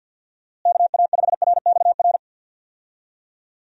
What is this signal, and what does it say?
Morse code, heard as a keyed tone, about 51 wpm, 700 Hz. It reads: KA5RXR
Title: USA Call Signs - 6 Characters - Repeated 3x - 50wpm